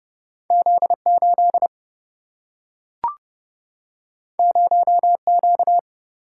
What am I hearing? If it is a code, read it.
Morse code, heard as a keyed tone, about 30 wpm, 700 Hz. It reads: Z 8 0 Q
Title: Single-Letter Single-Number 30wpm